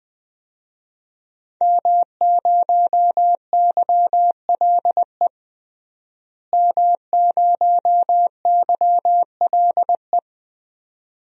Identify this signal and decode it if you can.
Morse code, heard as a keyed tone, about 20 wpm, 700 Hz. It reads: M0YLE M0YLE
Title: DXCC-100 Call Signs - 5 Characters - Repeated 2x - 20wpm